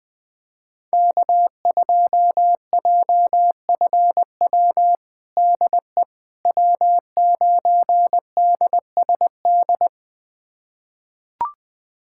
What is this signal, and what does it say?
Morse code, heard as a keyed tone, about 20 wpm, 700 Hz. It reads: K2JFW DE W9DSD
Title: USA Callsign DE Callsign - 5 Characters - 20wpm